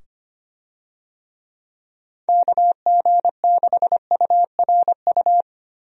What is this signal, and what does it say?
Morse code, heard as a keyed tone, about 25 wpm, 700 Hz. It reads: KG6URU
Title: USA Call Signs - 6 Characters - 25wpm